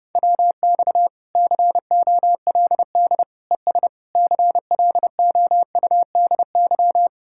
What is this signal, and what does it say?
Morse code, heard as a keyed tone, about 28 wpm, 700 Hz. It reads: WX COLD ES CLOUDY
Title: Speed-Racing Common QSO Phrases - 30wpm